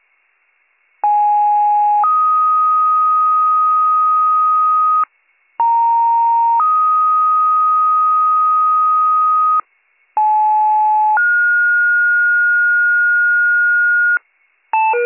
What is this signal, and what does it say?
Two-tone page, heard as a one-second tone, then a three-second tone, about 830 then 1250 Hz, about 930 then 1250 Hz, about 860 then 1430 Hz. It is heard from 1.0 to 14.2 s.